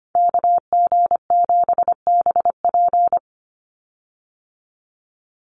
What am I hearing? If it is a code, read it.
Morse code, heard as a keyed tone, about 25 wpm, 700 Hz. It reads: KG7BP
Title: USA Call Signs 25wpm